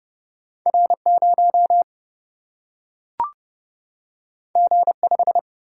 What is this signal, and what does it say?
Morse code, heard as a keyed tone, about 31 wpm, 700 Hz. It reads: R 0 G 5
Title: Single-Letter Single-Number 30wpm